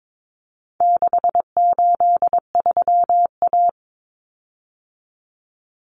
Morse code, heard as a keyed tone, about 22 wpm, 700 Hz. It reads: 683A
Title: Common Cut Numbers - 4-Digit 22wpm